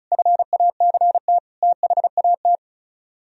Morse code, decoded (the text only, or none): FACT THAT